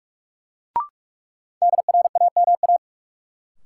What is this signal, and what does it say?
Morse code, heard as a keyed tone, about 43 wpm, 700 Hz. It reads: DRAMA
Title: Sentences from Top 1500 Words - New Word Drill Alt - 45wpm